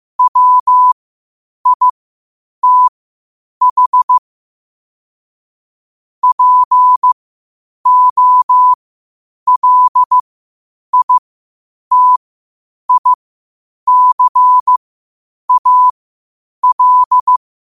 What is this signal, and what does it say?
Morse code, heard as a keyed tone, about 15 wpm, spaced as for 8 wpm, 1000 Hz. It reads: WITH POLITICAL
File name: qotdfiles / 8wpm